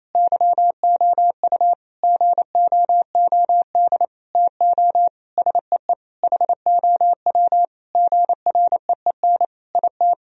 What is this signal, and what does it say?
Morse code, heard as a keyed tone, about 28 wpm, 700 Hz. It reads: YOU GOOD TO SEE HOW GREEN IT
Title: Sentences from Top 1000 Words - Review Part 2 - 28wpm